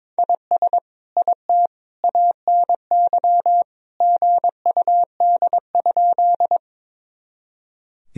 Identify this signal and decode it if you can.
Morse code, heard as a keyed tone, about 22 wpm, 700 Hz. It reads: IS IT ANY GUD?
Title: Sentences from Top 200 Words and Abbreviations 22wpm